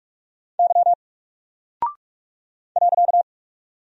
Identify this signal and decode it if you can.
Morse code, heard as a keyed tone, about 44 wpm, 700 Hz. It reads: Y .
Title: Single Character (Letter:Number:Punctuation:Prosign) Alt 45wpm